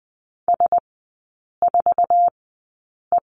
Morse code, heard as a keyed tone, about 20 wpm, spaced as for 8 wpm, 700 Hz. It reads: S4E
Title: 060 Lesson 11e - Letter h - Three Characters 20wpm